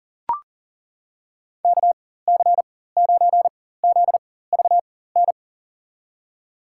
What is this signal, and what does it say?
Morse code, heard as a keyed tone, about 40 wpm, spaced as for 19 wpm, 700 Hz. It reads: KC9ZVN
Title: USA Call Signs - 6 Characters - 4x Character Spacing - 40wpm